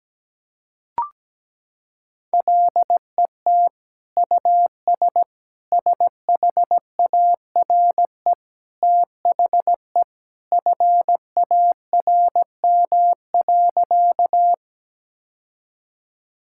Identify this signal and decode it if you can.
Morse code, heard as a keyed tone, about 17 wpm, 700 Hz. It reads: LET US SHARE THE FARM.
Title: Sentences from Top 1000 Words - Review Part 6 - 17wpm